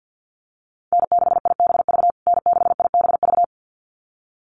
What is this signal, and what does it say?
Morse code, heard as a keyed tone, about 50 wpm, 700 Hz. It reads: N6IBV N6IBV
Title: USA Call Signs Repeated 2x 50wpm